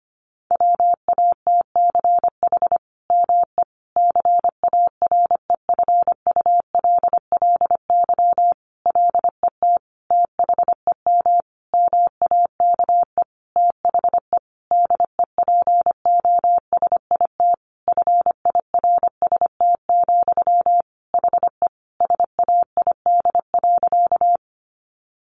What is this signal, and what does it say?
Morse code, heard as a keyed tone, about 25 wpm, 700 Hz. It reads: WATCH ME CAREFULLY LET THEM MAKE THE DEPOSIT FIRST, HE SAID.
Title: Sentences from Top 1300 Words 25wpm